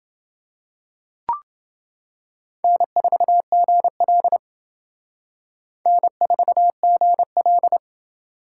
Morse code, heard as a keyed tone, about 29 wpm, 700 Hz. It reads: N4GL N4GL